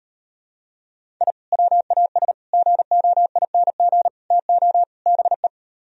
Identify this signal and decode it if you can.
Morse code, heard as a keyed tone, about 38 wpm, 700 Hz. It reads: I WAS GOING TO BE